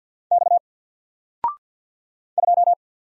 Morse code, heard as a keyed tone, about 48 wpm, 700 Hz. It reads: X 2
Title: Single Letter-Number Alt 50wpm